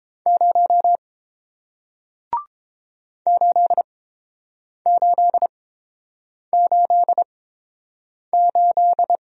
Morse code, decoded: 0 8888

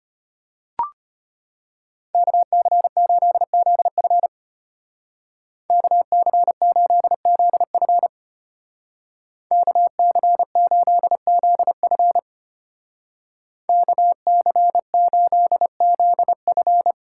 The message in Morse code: KC8ZF KC8ZF KC8ZF KC8ZF

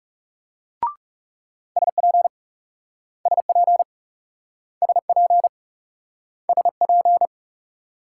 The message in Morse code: SP SP SP SP

